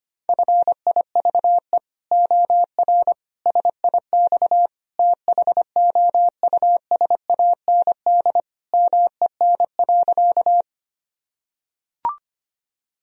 Morse code, heard as a keyed tone, about 25 wpm, 700 Hz. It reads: FIVE OR SIX THOUSAND MEN.
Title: Sentences from Top 1000 Words - Review Part 3 - 25wpm